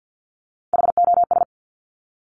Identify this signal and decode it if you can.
Morse code, heard as a keyed tone, about 50 wpm, 700 Hz. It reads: HOS
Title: Top Three Letter Combinations in Words 50wpm